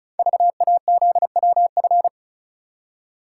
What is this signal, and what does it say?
Morse code, heard as a keyed tone, about 36 wpm, 690 Hz. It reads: VAZWF
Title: Code Groups - 5 Letter:Numbers - 35wpm